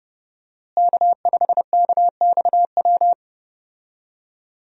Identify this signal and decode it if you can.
Morse code, heard as a keyed tone, about 30 wpm, 700 Hz. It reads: K5KXW
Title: USA Call Signs - 5 Characters - Repeated 3x - 30wpm